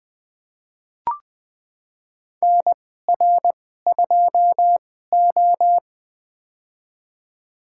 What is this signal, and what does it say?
Morse code, heard as a keyed tone, about 20 wpm, spaced as for 15 wpm, 700 Hz. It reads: NR2O
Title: USA Call Signs - 4 Characters - 2x Character Spacing - 20wpm